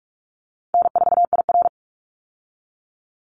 Morse code, heard as a keyed tone, about 44 wpm, 700 Hz. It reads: N4IR
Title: Call Signs Repeated 3x - 45wpm